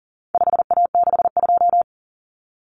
Morse code, heard as a keyed tone, about 40 wpm, 700 Hz. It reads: 5A62
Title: Common Cut Numbers - 4-Digit 40wpm